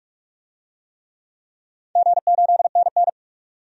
Morse code, heard as a keyed tone, about 48 wpm, 700 Hz. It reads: G8NN
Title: DXCC-100 Call Signs - 4 Characters - 45wpm